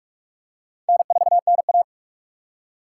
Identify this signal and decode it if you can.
Morse code, heard as a keyed tone, about 46 wpm, 700 Hz. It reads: N4NA
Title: USA Call Signs - 4 Characters - 4x Character Spacing - 45wpm